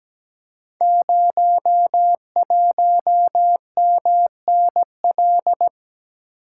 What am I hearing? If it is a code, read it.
Morse code, heard as a keyed tone, about 17 wpm, 700 Hz. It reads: 01MNL
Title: Code Groups - 5 Letter:Numbers - 17wpm